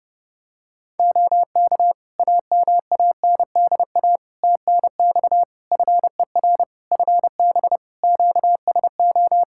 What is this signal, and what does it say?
Morse code, heard as a keyed tone, about 30 wpm, 700 Hz. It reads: OK AMANDA TNX FER FB QSO
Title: K1USN SST QSO Practice 30wpm